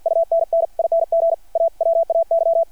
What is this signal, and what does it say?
Morse code, heard as a keyed tone, about 52 wpm, 660 Hz. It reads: UNNING AWAY
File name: archived / CRSnet